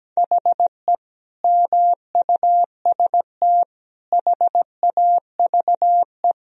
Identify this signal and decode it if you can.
Morse code, heard as a keyed tone, about 17 wpm, 700 Hz. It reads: HE MUST HAVE
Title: Sentences from Top 1000 Words - Review Part 4 - 17wpm